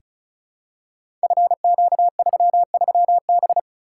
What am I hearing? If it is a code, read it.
Morse code, heard as a keyed tone, about 34 wpm, 700 Hz. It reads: FQ33B
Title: Code Groups - 5 Letter:Numbers - 35wpm